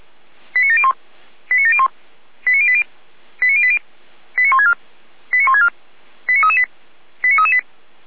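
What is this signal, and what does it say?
Five-tone selective call (ZVEI), heard as quick runs of five tones, tones of about 70 ms. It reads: 89701 89701 89080 89080 89164 89164 89338 89338
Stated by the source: ZVEI five-tone selective call received by SDR.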